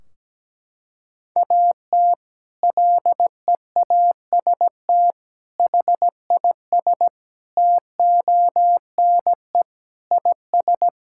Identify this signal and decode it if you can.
Morse code, heard as a keyed tone, about 17 wpm, 700 Hz. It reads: AT LEAST HIS TONE IS
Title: Sentences from Top 1600 Words 17wpm